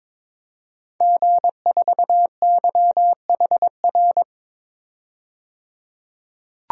Morse code, heard as a keyed tone, about 21 wpm, 700 Hz. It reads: G4YHR
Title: DXCC-100 Call Signs - 5 Characters - 22wpm